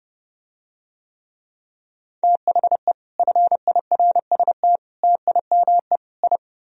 Morse code, heard as a keyed tone, about 30 wpm, 700 Hz. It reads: THE FIRST TIME I